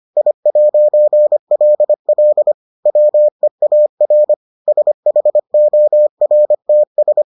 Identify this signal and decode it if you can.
Morse code, heard as a keyed tone, about 25 wpm, 600 Hz. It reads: I'LL WEAR SHORTS